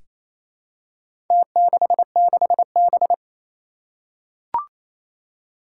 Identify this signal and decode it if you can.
Morse code, heard as a keyed tone, about 28 wpm, 700 Hz. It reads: T66B
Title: Four Challenging Letter-Number Combinations 28wpm